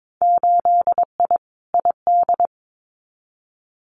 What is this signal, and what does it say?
Morse code, heard as a keyed tone, about 22 wpm, 700 Hz. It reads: 8I ID
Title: Winter Field Day Exchanges Encore 22wpm